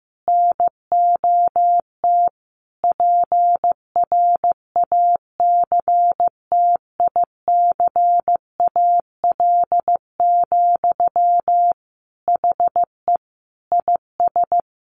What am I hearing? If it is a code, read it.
Morse code, heard as a keyed tone, about 14 wpm, 700 Hz. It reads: NOT PRACTICAL, HE IS
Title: Sentences from Top 1400 Words 15wpm